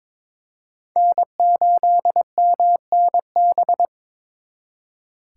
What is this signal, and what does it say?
Morse code, heard as a keyed tone, about 26 wpm, 700 Hz. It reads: N8MNB
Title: Speed Racing - USA 1x3 Call Signs 22wpm